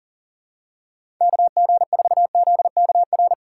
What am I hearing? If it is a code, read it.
Morse code, heard as a keyed tone, about 40 wpm, 700 Hz. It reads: KG4ZKR
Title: USA Call Signs - 6 Characters - 6x Character Spacing - 40wpm